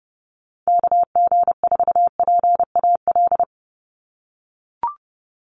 Morse code, heard as a keyed tone, about 29 wpm, 700 Hz. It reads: KG4PAL
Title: USA Call Signs 30wpm